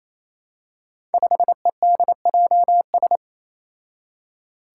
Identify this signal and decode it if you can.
Morse code, heard as a keyed tone, about 28 wpm, 700 Hz. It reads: 5EDJS
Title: Five Challenging Letter-Number Combinations 28wpm